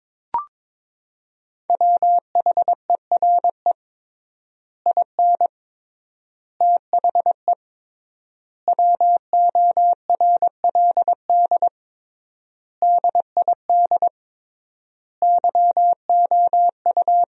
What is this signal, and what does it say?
Morse code, heard as a keyed tone, about 21 wpm, 700 Hz. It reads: WHERE IN THE WORLD DID YOU
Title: 3x Word-Spacing - Sentences from Top 200 Words - 22wpm